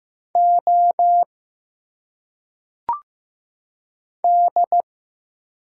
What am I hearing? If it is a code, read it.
Morse code, heard as a keyed tone, about 15 wpm, 700 Hz. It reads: O D
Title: Single Letters - Phonetic Pronunciation 15wpm